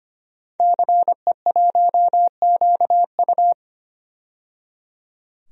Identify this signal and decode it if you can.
Morse code, heard as a keyed tone, about 25 wpm, 700 Hz. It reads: CE1QU
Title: DXCC-100 Call Signs - 5 Characters - 25wpm